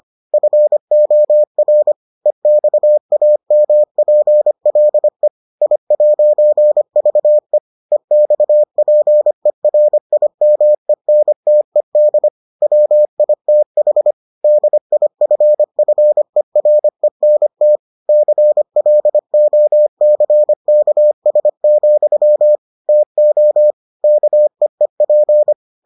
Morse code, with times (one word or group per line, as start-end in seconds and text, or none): 0.3-1.9 FOR
2.3-5.3 EXAMPLE
5.6-7.6 I'VE
7.9-12.3 EXPERIMENTED
12.6-14.1 WITH
14.4-17.8 DIFFERENT
18.1-22.6 CLOCKS,
22.9-23.7 TO
24.0-25.5 KEEP